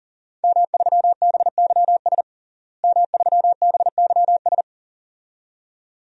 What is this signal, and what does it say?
Morse code, heard as a keyed tone, about 40 wpm, 700 Hz. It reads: M3BYS M3BYS
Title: DXCC-100 Call Signs - 5 Characters - Repeated 2x - 40wpm